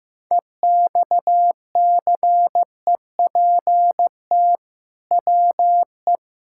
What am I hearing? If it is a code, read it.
Morse code, heard as a keyed tone, about 15 wpm, 700 Hz. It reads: EXCEPT WE
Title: Sentences from Top 1600 Words 15wpm